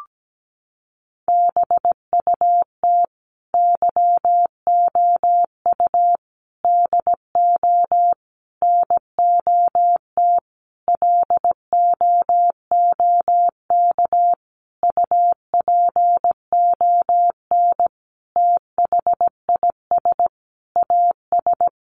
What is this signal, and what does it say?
Morse code, heard as a keyed tone, about 17 wpm, 700 Hz. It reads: BUT YOU DO NOT LOOK UPON THIS AS
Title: Sentences from Top 1100 Words 17wpm